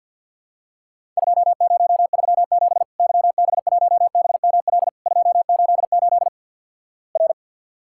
Morse code, heard as a keyed tone, about 50 wpm, 700 Hz. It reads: 2037 YB1BML 288 R
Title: Worked All Europe DX Contest - QTC Exchange - 50wpm